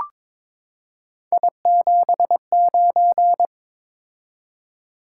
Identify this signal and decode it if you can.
Morse code, heard as a keyed tone, about 22 wpm, 700 Hz. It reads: I79
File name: practice / 22wpm / Three Letter-Number Combinations - 22wpm